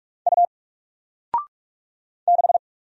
Morse code, heard as a keyed tone, about 46 wpm, 700 Hz. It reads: U 6
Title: Single Letter-Number 45wpm